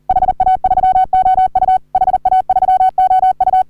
Morse code, heard as a keyed tone, about 40 wpm, 750 Hz. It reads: HA3OU HA3OU
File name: CW / HA3OU RHOBCW REC